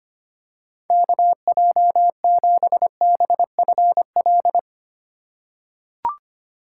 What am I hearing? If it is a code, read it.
Morse code, heard as a keyed tone, about 25 wpm, 700 Hz. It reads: KJ7BFL
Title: USA Call Signs - 6 Characters - 4x Character Spacing - 25wpm